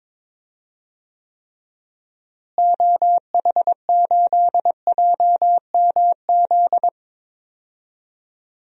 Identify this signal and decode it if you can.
Morse code, heard as a keyed tone, about 21 wpm, 700 Hz. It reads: OH8JMZ
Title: DXCC-100 Call Signs - 6 Characters - Repeated 2x - 22wpm